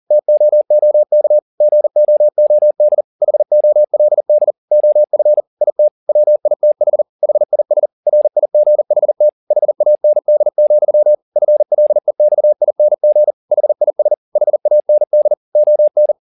Morse code, heard as a keyed tone, about 40 wpm, 600 Hz. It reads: TOOK GOOD HOLD OF IT WITH HIS RIGHT HAND, FLEXING HIS HAND ON
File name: the old man and the sea 2 40wpm